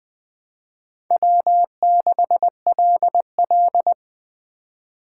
Morse code, heard as a keyed tone, about 20 wpm, 700 Hz. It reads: W6LL
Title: USA Call Signs - 4 Characters - 2x Character Spacing - 20wpm